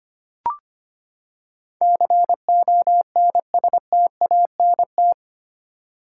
Morse code, heard as a keyed tone, about 25 wpm, 700 Hz. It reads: CONSTANT
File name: Eight Letter Words - 25wpm